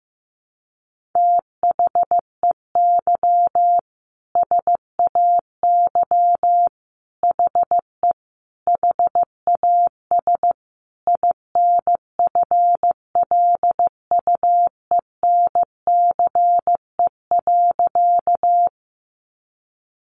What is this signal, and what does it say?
Morse code, heard as a keyed tone, about 15 wpm, 700 Hz. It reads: THEY SAY HE HAS INFLUENCE.
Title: Sentences from Top 1100 Words 15wpm